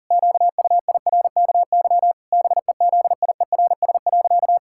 Morse code, heard as a keyed tone, about 40 wpm, 700 Hz. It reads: QUIRKY BEZIERS.
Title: Pangram Challenge v2 - 40wpm